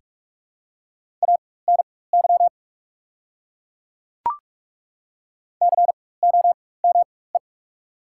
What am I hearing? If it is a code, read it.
Morse code, heard as a keyed tone, about 41 wpm, spaced as for 20 wpm, 700 Hz. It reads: ANY COME
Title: 4x Character Spacing - Top 300 Words - 45wpm